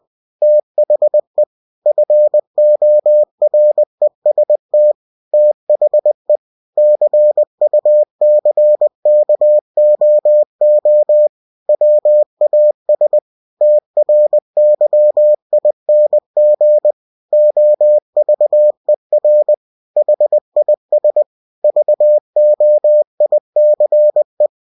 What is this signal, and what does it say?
Morse code, heard as a keyed tone, about 20 wpm, 600 Hz. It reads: THE FOREST THE CUCKOO WAS TRYING OVER HIS VOICE
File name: cw-books / winnie-the-pooh / winnie-20wpm-page-0101